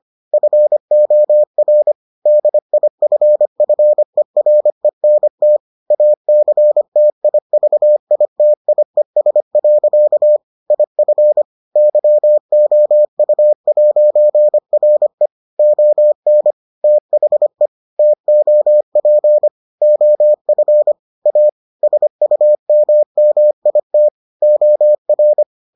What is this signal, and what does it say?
Morse code, heard as a keyed tone, about 25 wpm, 600 Hz. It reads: FOR DIFFERENT ACTIVITIES. IF YOU'RE ON THE TOP OF A SUMMIT OR